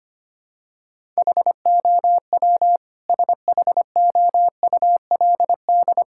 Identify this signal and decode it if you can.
Morse code, heard as a keyed tone, about 25 wpm, 700 Hz. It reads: HOW SHOULD
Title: Sentences from Top 1000 Words - Review Part 5 - 25wpm